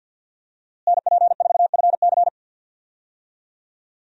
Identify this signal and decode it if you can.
Morse code, heard as a keyed tone, about 53 wpm, 700 Hz. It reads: NP4FC
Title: USA Call Signs - 5 Characters - Repeated 3x - 50wpm